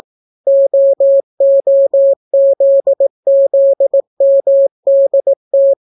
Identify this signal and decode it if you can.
Morse code, heard as a keyed tone, about 18 wpm, 550 Hz. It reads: OOZZMDT